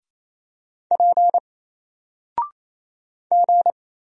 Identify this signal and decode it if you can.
Morse code, heard as a keyed tone, about 27 wpm, 700 Hz. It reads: P G
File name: Single-Letter-Number-28wpm